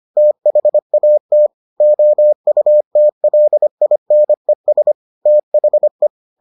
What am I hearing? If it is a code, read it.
Morse code, heard as a keyed tone, about 25 wpm, 600 Hz. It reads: THAT OUTLINES THE